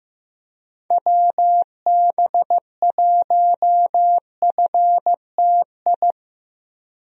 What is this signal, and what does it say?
Morse code, heard as a keyed tone, about 15 wpm, 700 Hz. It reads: WB1FTI
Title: USA Call Signs - 6 Characters - 15wpm